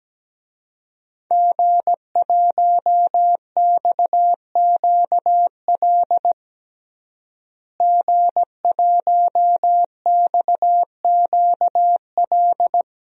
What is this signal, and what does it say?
Morse code, heard as a keyed tone, about 17 wpm, 700 Hz. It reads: G1XQL G1XQL
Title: DXCC-100 Call Signs - 5 Characters - Repeated 3x - 17wpm